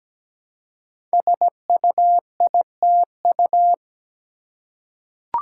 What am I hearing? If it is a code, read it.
Morse code, heard as a keyed tone, about 17 wpm, 700 Hz. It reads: SUITU
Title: Code Groups - 5 Letters - 17wpm